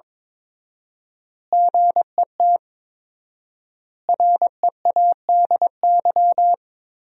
Morse code, heard as a keyed tone, about 22 wpm, 700 Hz. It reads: GET READY
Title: 4x Word-Spacing - Sentences from Top 400 Words - 22wpm